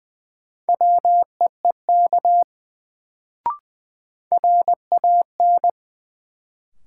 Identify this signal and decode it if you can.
Morse code, heard as a keyed tone, about 20 wpm, 700 Hz. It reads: WEEK RAN
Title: Sentences from Top 500 Words - New Word Drill - 20wpm